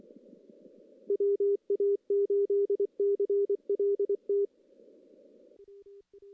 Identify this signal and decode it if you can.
Morse code, heard as a keyed tone, about 28 wpm, 400 Hz. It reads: WA8CLT
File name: CW / CW-SS-k5tr- -1700